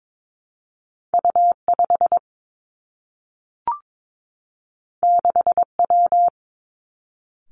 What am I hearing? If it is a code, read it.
Morse code, heard as a keyed tone, about 22 wpm, 700 Hz. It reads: U 5 6 W
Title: Two Challenging Letter-Number Combinations 22wpm